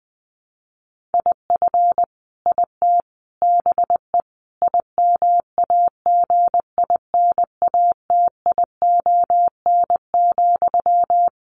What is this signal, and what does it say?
Morse code, heard as a keyed tone, about 19 wpm, 700 Hz. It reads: IF IT BE IMAGINATION,
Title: Sentences from Top 1300 Words 20wpm